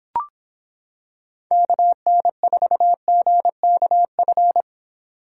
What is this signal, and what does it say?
Morse code, heard as a keyed tone, about 26 wpm, 700 Hz. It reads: KN4GKF